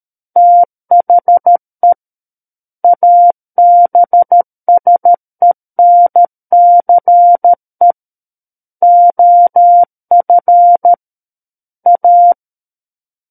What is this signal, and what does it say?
Morse code, heard as a keyed tone, about 13 wpm, 700 Hz. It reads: THE ABSENCE OF A
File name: Book One / 04-Chapter 4 - 13 WPM